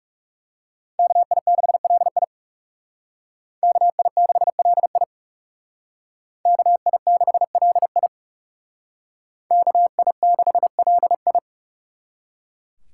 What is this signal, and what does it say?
Morse code, heard as a keyed tone, about 37 wpm, 700 Hz. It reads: KI6LI KI6LI KI6LI KI6LI
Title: Speed Racing - USA Call Signs 30wpm